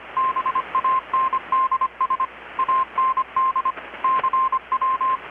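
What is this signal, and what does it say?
Morse code, heard as a keyed tone, about 25 wpm, 1050 Hz, with radio noise. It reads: BANDS AND CW
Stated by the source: Morse from KPH's Night of Nights XX maritime radiotelegraphy event, received in Chiapas, Mexico about 3,800 km away on a Kenwood R-600 with a grounded 10 m long-wire antenna.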